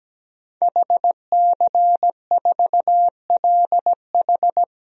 Morse code, heard as a keyed tone, about 17 wpm, 700 Hz. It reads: HC4LH
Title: DXCC-100 Call Signs - 5 Characters - 17wpm